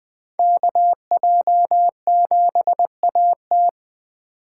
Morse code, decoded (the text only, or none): KJ7AT